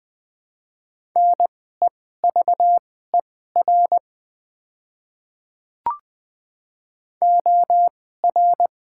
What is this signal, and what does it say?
Morse code, heard as a keyed tone, about 21 wpm, spaced as for 15 wpm, 700 Hz. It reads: NEVER OR
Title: 2x Character Spacing - Top 500 Words - 20wpm